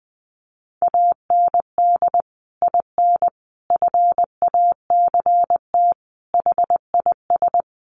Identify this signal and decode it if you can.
Morse code, heard as a keyed tone, about 20 wpm, 700 Hz. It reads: AND IN FACT HIS
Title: Sentences from Top 1000 Words 20wpm